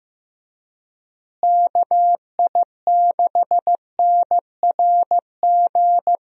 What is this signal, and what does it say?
Morse code, heard as a keyed tone, about 15 wpm, 700 Hz. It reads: KI6NRG
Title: USA Call Signs - 6 Characters - Repeated 3x - 15wpm